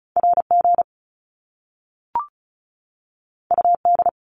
Morse code, heard as a keyed tone, about 36 wpm, 700 Hz. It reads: RG UD
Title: Top Two Letter Combinations in Words 35wpm